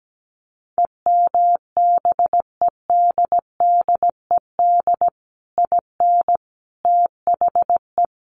Morse code, heard as a keyed tone, about 17 wpm, 700 Hz. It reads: EMBEDDED IN THE
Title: A Sound Of Thunder 17wpm